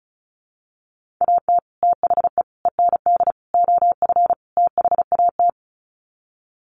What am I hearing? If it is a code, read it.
Morse code, heard as a keyed tone, about 34 wpm, 700 Hz. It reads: AT THE END OF THAT
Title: Sets of 5 Words Encore2 35wpm